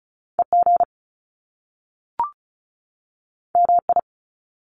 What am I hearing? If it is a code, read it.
Morse code, heard as a keyed tone, about 36 wpm, 700 Hz. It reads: EG MI
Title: Top Two Letter Combinations in Words 35wpm